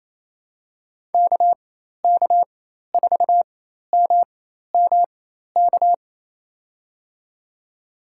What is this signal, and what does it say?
Morse code, heard as a keyed tone, about 28 wpm, spaced as for 13 wpm, 700 Hz. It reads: KK4MMK
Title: USA Call Signs - 6 Characters - 4x Character Spacing - 28wpm